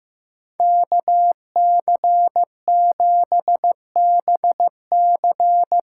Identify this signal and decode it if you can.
Morse code, heard as a keyed tone, about 15 wpm, 700 Hz. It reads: KC7BC
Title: USA Call Signs - 6x Character Spacing - 15wpm